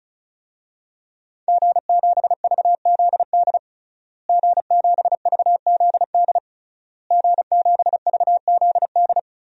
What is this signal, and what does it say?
Morse code, heard as a keyed tone, about 36 wpm, 700 Hz. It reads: G7VZD G7VZD G7VZD
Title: DXCC-100 Call Signs - 5 Characters - Repeated 3x - 35wpm